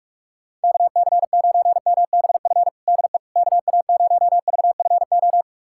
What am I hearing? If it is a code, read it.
Morse code, heard as a keyed tone, about 46 wpm, 700 Hz. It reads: KC9MBF DE KA0VFO
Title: USA Callsign DE Callsign - 6 Characters - 45wpm